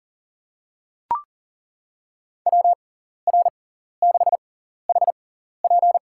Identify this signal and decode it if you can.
Morse code, heard as a keyed tone, about 40 wpm, spaced as for 14 wpm, 700 Hz. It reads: WR6HP
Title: USA Call Signs - 5 Characters - 6x Character Spacing - 40wpm